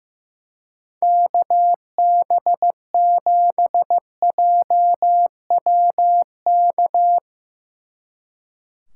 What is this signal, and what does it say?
Morse code, heard as a keyed tone, about 17 wpm, 700 Hz. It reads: KB7JWK
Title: Speed Racing - USA 2x3 Call Signs 15wpm